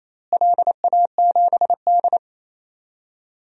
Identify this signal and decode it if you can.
Morse code, heard as a keyed tone, about 28 wpm, 700 Hz. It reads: LA7D
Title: DXCC-100 Call Signs - 4 Characters - Repeated 2x - 28wpm